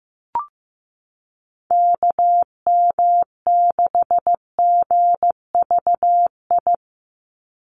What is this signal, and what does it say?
Morse code, heard as a keyed tone, about 15 wpm, 700 Hz. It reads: KM6GVI
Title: USA Call Signs 15wpm